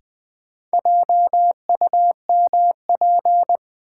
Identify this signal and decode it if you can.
Morse code, heard as a keyed tone, about 20 wpm, 700 Hz. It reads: JUMP
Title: Sentences from Top 1300 Words - New Word Drill Alt - 20wpm